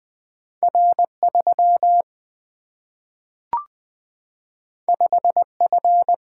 Morse code, heard as a keyed tone, about 20 wpm, 700 Hz. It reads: R 3 5 F
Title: Single-Letter Single-Number 20wpm